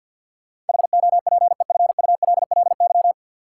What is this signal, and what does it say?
Morse code, heard as a keyed tone, about 51 wpm, 700 Hz. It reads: HOPEFULLY
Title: Sentences from Top 1500 Words - New Word Drill Alt - 50wpm